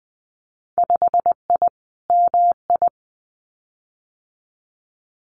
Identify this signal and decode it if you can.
Morse code, heard as a keyed tone, about 20 wpm, 700 Hz. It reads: 5I MI
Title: Winter Field Day Exchanges Encore 20wpm